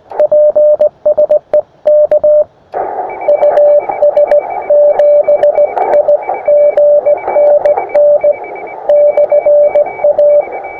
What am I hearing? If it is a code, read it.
Morse code, heard as a keyed tone, about 19 wpm, spaced as for 16 wpm, 600 Hz, with radio noise. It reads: PSE K US7IGN N/A
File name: attachments / us7ign a 20m